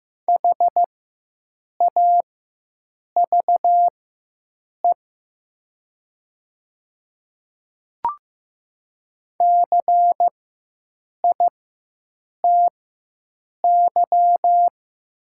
Morse code, heard as a keyed tone, about 15 wpm, spaced as for 7 wpm, 700 Hz. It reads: HAVE CITY
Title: 4x Character Spacing - Top 300 Words - 15wpm